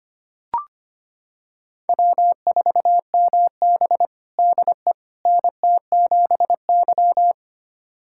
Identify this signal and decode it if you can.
Morse code, heard as a keyed tone, about 25 wpm, 700 Hz. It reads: W4MB DE NT7Y
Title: USA Callsign DE Callsign - 4 Characters - 25wpm